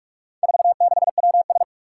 Morse code, heard as a keyed tone, about 46 wpm, 700 Hz. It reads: 46WS
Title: Four Challenging Letter-Number Combinations 45wpm